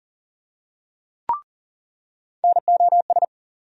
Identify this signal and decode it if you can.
Morse code, heard as a keyed tone, about 40 wpm, 700 Hz. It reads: NOS
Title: USA Call Sign Postfixes - 3 Letters 40wpm